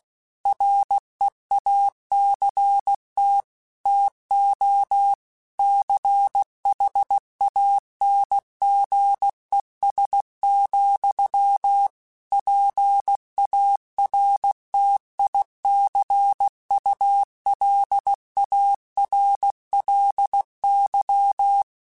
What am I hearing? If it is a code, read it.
Morse code, heard as a keyed tone, about 16 wpm, 800 Hz. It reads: REACT TO CHANGES, PARTICULARLY
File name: Technical Subjects / Technical Subjects-mp3s / Reactance @15